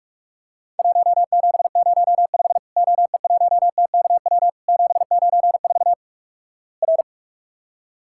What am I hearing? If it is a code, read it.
Morse code, heard as a keyed tone, about 44 wpm, 700 Hz. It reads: 1705 OE1TKW 794 R
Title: Worked All Europe DX Contest - QTC Exchange - 45wpm